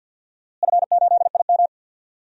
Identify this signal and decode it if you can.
Morse code, heard as a keyed tone, about 51 wpm, 700 Hz. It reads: F8IM
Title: DXCC-100 Call Signs - 4 Characters - Repeated 3x - 50wpm